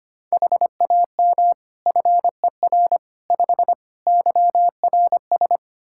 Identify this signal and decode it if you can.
Morse code, heard as a keyed tone, about 25 wpm, 700 Hz. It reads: HAM FER 5 YRS
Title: Practice QSO - Exchange 3 - More Info - 25wpm